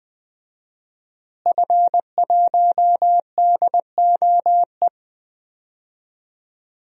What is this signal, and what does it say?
Morse code, heard as a keyed tone, about 20 wpm, 700 Hz. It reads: F1DOE
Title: DXCC-100 Call Signs - 5 Characters - Repeated 2x - 20wpm